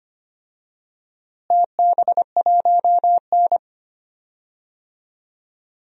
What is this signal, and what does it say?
Morse code, heard as a keyed tone, about 25 wpm, 700 Hz. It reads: TB1N
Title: DXCC-100 Call Signs - 4 Characters - Repeated 2x - 25wpm